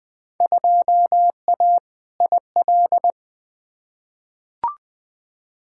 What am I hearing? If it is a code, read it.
Morse code, heard as a keyed tone, about 20 wpm, 700 Hz. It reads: 2A IL
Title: Field Day Exchanges - 20wpm